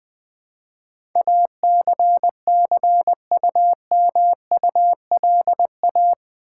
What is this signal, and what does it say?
Morse code, heard as a keyed tone, about 20 wpm, 700 Hz. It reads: ACCUMULA
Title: Build A Word - 4 Letters At A Time - 10-Letter Words - 20wpm